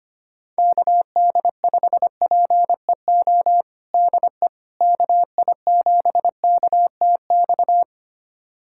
Morse code, heard as a keyed tone, about 25 wpm, 700 Hz. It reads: KD5PEO DE KI7KTX
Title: USA Callsign DE Callsign - 6 Characters - 25wpm